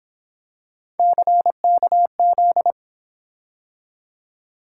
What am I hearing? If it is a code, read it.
Morse code, heard as a keyed tone, about 26 wpm, 700 Hz. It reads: CKZ
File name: Speed-Racing - Three Letter-Number Combinations - 22wpm